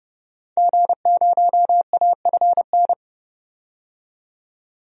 Morse code, heard as a keyed tone, about 30 wpm, 700 Hz. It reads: G0AFN
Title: DXCC-100 Call Signs - 5 Characters - 30wpm